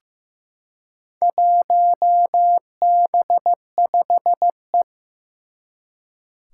Morse code, heard as a keyed tone, about 15 wpm, 700 Hz. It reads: 1B5E
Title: Four Challenging Letter-Number Combinations 15wpm